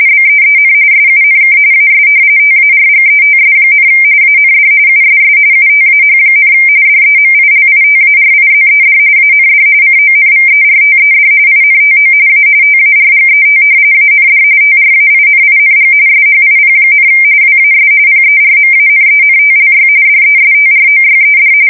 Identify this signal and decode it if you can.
RTTY, about 45.45 baud. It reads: BEEN SUNK AS TARGETS, AND USS UTAH, DEMILITARIZED AND CONVERTED INTO A TARGET AND TRAINING SHIP, WAS PERMANENTLY DESTROYED
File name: fsk170 / battleships